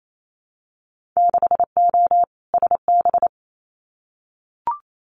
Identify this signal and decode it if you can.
Morse code, heard as a keyed tone, about 28 wpm, 690 Hz. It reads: 6O SB
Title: Winter Field Day Exchanges Encore 28wpm